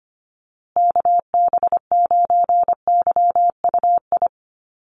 Morse code, heard as a keyed tone, about 25 wpm, 700 Hz. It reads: KB9YUI
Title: USA Call Signs Repeated 3x 25wpm